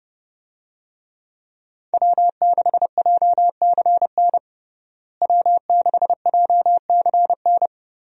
Morse code, heard as a keyed tone, about 32 wpm, 700 Hz. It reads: W6JCN W6JCN
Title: USA Call Signs - 5 Characters - Repeated 2x - 30wpm